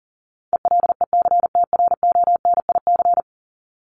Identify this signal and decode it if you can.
Morse code, heard as a keyed tone, about 38 wpm, 700 Hz. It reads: ELECTRONIC
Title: Top 1001-1250 Words 40wpm